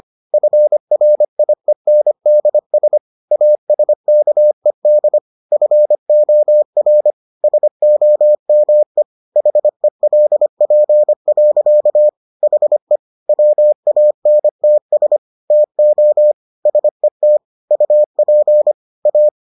Morse code, heard as a keyed tone, about 25 wpm, 600 Hz. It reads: FRIENDS ASKED FOR SOME HELP. HE WANTS TO SET UP A